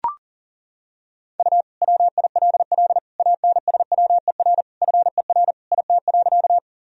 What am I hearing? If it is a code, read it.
Morse code, heard as a keyed tone, about 40 wpm, 700 Hz. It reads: U WILL ANSWER FER IT.
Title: Sentences from Top 300 Words and Abbreviations 40wpm